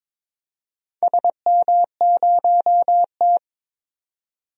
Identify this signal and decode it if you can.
Morse code, heard as a keyed tone, about 22 wpm, 700 Hz. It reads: SM0T
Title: DXCC-100 Call Signs - 4 Characters - 22wpm